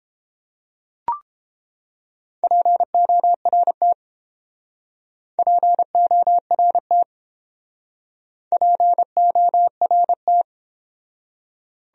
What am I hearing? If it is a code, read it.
Morse code, heard as a keyed tone, about 31 wpm, 700 Hz. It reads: PORT PORT PORT